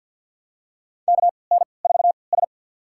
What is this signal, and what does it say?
Morse code, heard as a keyed tone, about 53 wpm, spaced as for 29 wpm, 700 Hz. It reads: KN4S
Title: USA Call Signs - 4 Characters - 3x Character Spacing - 50wpm